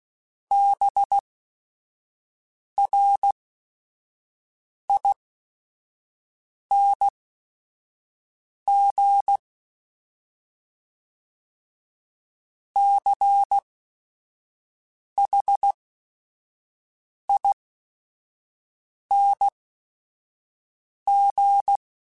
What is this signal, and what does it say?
Morse code, heard as a keyed tone, about 16 wpm, spaced as for 5 wpm, 800 Hz. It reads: BRING CHING